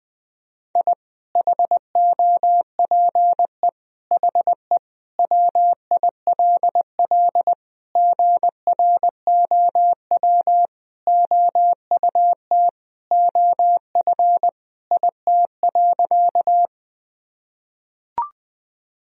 Morse code, heard as a keyed tone, about 20 wpm, 700 Hz. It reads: I HOPE HE WILL GROW OUT OF IT.
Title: Sentences from Top 1000 Words - Review Part 1 - 20wpm